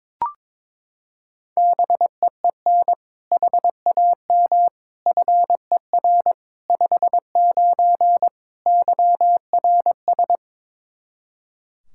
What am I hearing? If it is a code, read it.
Morse code, heard as a keyed tone, about 22 wpm, 700 Hz. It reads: BEEN HAM FER 59 YRS
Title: Practice QSO - Exchange 3 - More Info - 22wpm